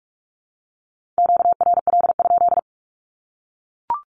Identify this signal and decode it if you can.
Morse code, heard as a keyed tone, about 44 wpm, 700 Hz. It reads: QRL?
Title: Q-Codes 45wpm